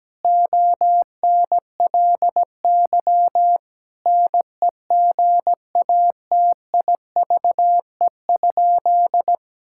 Morse code, heard as a keyed tone, about 17 wpm, 700 Hz. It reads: ONLY NEGATIVE?
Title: Sentences from Top 1000 Words - Review Part 6 - 17wpm